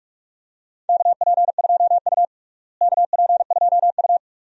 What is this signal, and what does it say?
Morse code, heard as a keyed tone, about 46 wpm, 700 Hz. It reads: KP2U KP2U
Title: DXCC-100 Call Signs - 4 Characters - Repeated 2x - 45wpm